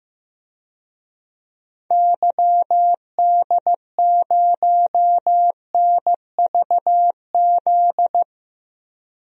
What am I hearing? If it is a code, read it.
Morse code, heard as a keyed tone, about 15 wpm, 700 Hz. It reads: YD0NVZ
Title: DXCC-100 Call Signs - 6 Characters - Repeated 3x - 15wpm